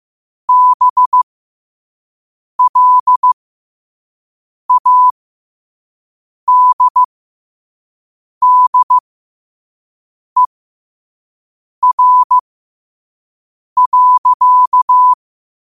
Morse code, heard as a keyed tone, about 15 wpm, spaced as for 5 wpm, 1000 Hz. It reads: BLADDER.
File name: qotdfiles / 5wpm